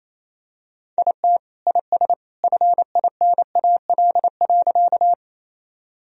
Morse code, heard as a keyed tone, about 28 wpm, 700 Hz. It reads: IT IS FINAL.
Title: Sentences from Top 1000 Words - Review Part 2 - 28wpm